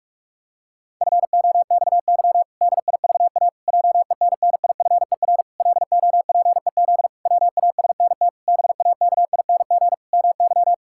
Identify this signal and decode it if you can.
Morse code, heard as a keyed tone, about 46 wpm, 700 Hz. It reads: FOXY DIVA JENNIFER LOPEZ WASNT BAKING MY
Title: Pangram Challenge v2 - 45wpm